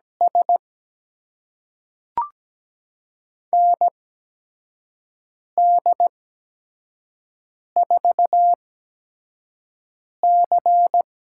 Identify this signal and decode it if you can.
Morse code, heard as a keyed tone, about 17 wpm, spaced as for 5 wpm, 700 Hz. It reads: S ND4C
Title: USA Call Signs - 4 Characters - 8x Character Spacing - 17wpm